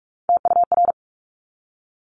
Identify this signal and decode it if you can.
Morse code, heard as a keyed tone, about 46 wpm, 700 Hz. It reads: TUR
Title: Top Three Letter Combinations in Words 45wpm